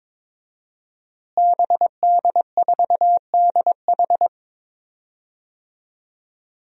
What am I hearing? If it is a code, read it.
Morse code, heard as a keyed tone, about 22 wpm, 700 Hz. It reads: BD4DH
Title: DXCC-100 Call Signs - 5 Characters - Repeated 2x - 22wpm